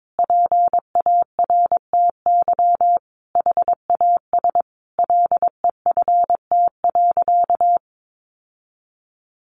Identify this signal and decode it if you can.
Morse code, heard as a keyed tone, about 22 wpm, 700 Hz. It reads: PARTY HAS LEFT.
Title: Sentences from Top 1500 Words 22wpm